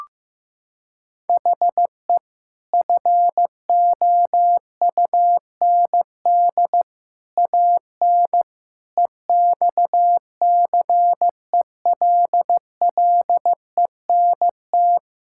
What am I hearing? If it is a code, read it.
Morse code, heard as a keyed tone, about 15 wpm, 700 Hz. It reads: HE FOUND AN EXCELLENT
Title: Sentences from Top 1000 Words - Review Part 6 - 15wpm